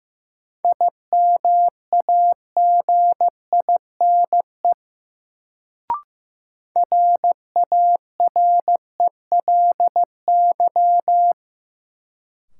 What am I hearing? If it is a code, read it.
Morse code, heard as a keyed tone, about 15 wpm, 700 Hz. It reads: IMAGINE RARELY
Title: Sentences from Top 1400 Words - New Word Drill - 15wpm